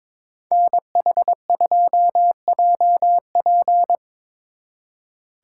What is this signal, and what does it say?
Morse code, heard as a keyed tone, about 26 wpm, 700 Hz. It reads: NH2JP
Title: Speed Racing - USA 2x2 Call Signs 22wpm